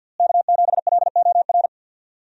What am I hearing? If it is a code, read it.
Morse code, heard as a keyed tone, about 52 wpm, 700 Hz. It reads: K7LOR
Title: USA Call Signs - 5 Characters - Repeated 2x - 50wpm